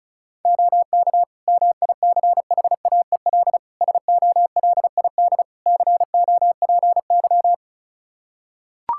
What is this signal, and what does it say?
Morse code, heard as a keyed tone, about 34 wpm, 700 Hz. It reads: OK MICHAEL SOLID COPY
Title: K1USN SST QSO Practice 35wpm